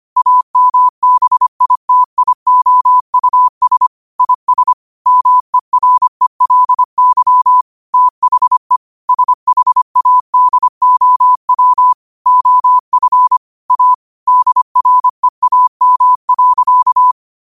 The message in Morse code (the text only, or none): AMBITIOUS IS MERELY THE SHADOW OF A DREAM.